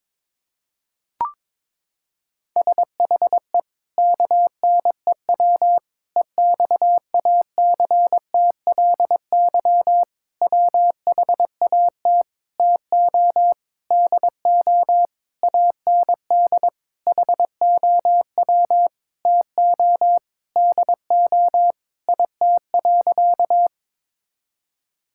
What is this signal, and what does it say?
Morse code, heard as a keyed tone, about 22 wpm, 700 Hz. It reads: SHE KNEW EXACTLY WHAT TO DO AND HOW TO DO IT.
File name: practice / 22wpm / Sentences with 8-Word Phrases - 22wpm